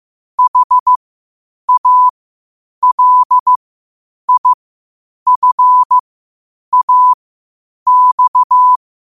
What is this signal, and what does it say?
Morse code, heard as a keyed tone, about 15 wpm, spaced as for 8 wpm, 1000 Hz. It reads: HALIFAX
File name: qotdfiles / 8wpm